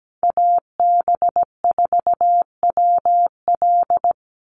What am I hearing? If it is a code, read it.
Morse code, heard as a keyed tone, about 17 wpm, 700 Hz. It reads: AB4WL
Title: NA SOTA Chaser Callsigns 17wpm